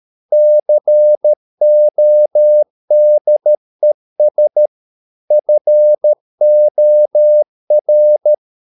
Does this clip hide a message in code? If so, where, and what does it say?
Yes: Morse code from 0.3 to 8.3 s, heard as a keyed tone, about 13 wpm, 600 Hz. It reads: CODES FOR